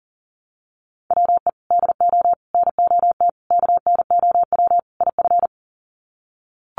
Morse code, heard as a keyed tone, about 40 wpm, 700 Hz. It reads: WE DO NOT KNOW IF
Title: Sets of 5 Words Encore2 40wpm